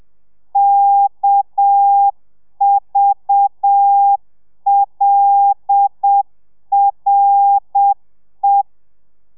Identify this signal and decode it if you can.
Morse code, heard as a keyed tone, about 7 wpm, 800 Hz. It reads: KVLRE